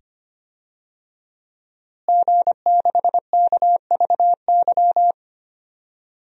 Morse code, heard as a keyed tone, about 25 wpm, 700 Hz. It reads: G6KVY
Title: DXCC-100 Call Signs - 5 Characters - Repeated 2x - 25wpm